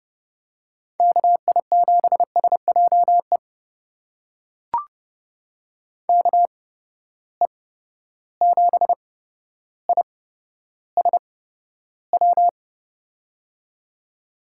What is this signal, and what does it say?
Morse code, heard as a keyed tone, about 30 wpm, spaced as for 13 wpm, 700 Hz. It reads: KI7SJE KE7ISW
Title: USA Call Signs - 6 Characters - 8x Character Spacing - 30wpm